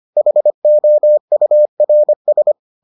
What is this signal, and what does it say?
Morse code, heard as a keyed tone, about 25 wpm, 600 Hz. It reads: HOURS